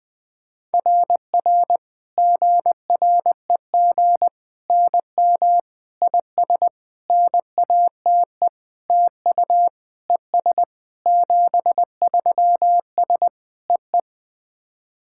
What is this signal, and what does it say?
Morse code, heard as a keyed tone, about 20 wpm, 690 Hz. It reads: RR GREG NM IS NATE TU ES 73S EE
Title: SOTA QSO Practice 20wpm